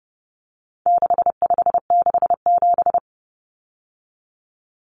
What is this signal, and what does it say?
Morse code, heard as a keyed tone, about 30 wpm, 700 Hz. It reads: 6567
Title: Numbers - 4-Digit 30wpm